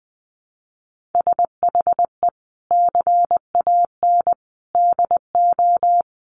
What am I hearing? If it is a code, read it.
Morse code, heard as a keyed tone, about 20 wpm, 700 Hz. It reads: SHE CAN DO
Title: Sentences from Top 1000 Words 20wpm